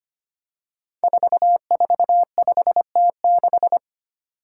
Morse code, heard as a keyed tone, about 25 wpm, 700 Hz. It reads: 445T6
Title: Five Challenging Letter-Number Combinations 25wpm